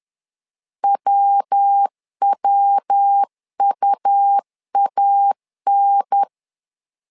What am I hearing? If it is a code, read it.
Morse code, heard as a keyed tone, about 10 wpm, 800 Hz. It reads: WWUAN
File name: LEZIONI / 3 MWRO / 3 MW 10wpm 1959